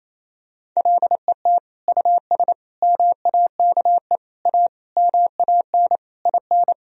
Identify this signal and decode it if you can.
Morse code, heard as a keyed tone, about 28 wpm, 700 Hz. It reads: LET US MAKE A MAN IN
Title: Sentences from Top 1000 Words - Review Part 4 - 28wpm